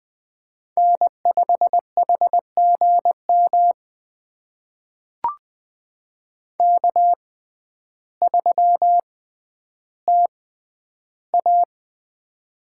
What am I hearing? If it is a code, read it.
Morse code, heard as a keyed tone, about 20 wpm, spaced as for 11 wpm, 700 Hz. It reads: N5HGM K3TA
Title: USA Call Signs - 6x Character Spacing - 20wpm